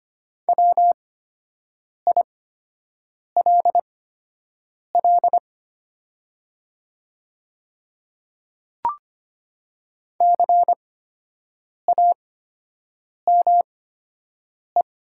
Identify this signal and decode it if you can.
Morse code, heard as a keyed tone, about 25 wpm, spaced as for 7 wpm, 700 Hz. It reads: WILL CAME
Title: 8x Character Spacing - Top 300 Words - 25wpm